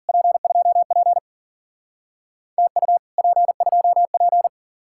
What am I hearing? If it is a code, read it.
Morse code, heard as a keyed tone, about 52 wpm, 700 Hz. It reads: P2P TU P2P